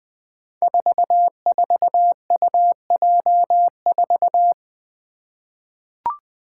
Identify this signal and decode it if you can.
Morse code, heard as a keyed tone, about 20 wpm, 700 Hz. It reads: 44UJ4
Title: Five Challenging Letter-Number Combinations 20wpm